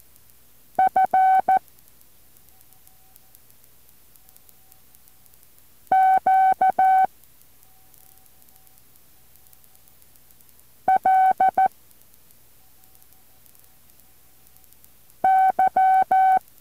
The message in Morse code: F Q L Y